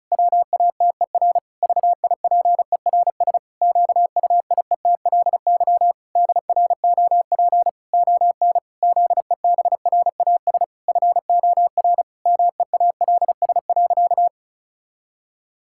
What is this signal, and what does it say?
Morse code, heard as a keyed tone, about 36 wpm, 700 Hz. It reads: WATER VIPERS QUIETLY DROP ON ZEBRAS FOR MEALS.
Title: Pangram Challenge v2 - 35wpm